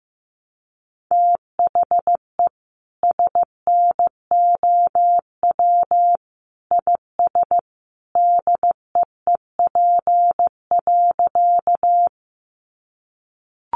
Morse code, heard as a keyed tone, about 14 wpm, 700 Hz. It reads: THE SNOW IS DEEP.
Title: Sentences from Top 500 Words 15wpm